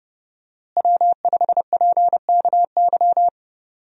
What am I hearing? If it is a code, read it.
Morse code, heard as a keyed tone, about 31 wpm, 700 Hz. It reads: W5PKY
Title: USA Call Signs - 5 Characters - Repeated 3x - 30wpm